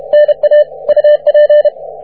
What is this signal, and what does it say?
Morse code, heard as a keyed tone, about 32 wpm, 590 Hz. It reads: NA UP
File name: wp / ft5ga 17m cw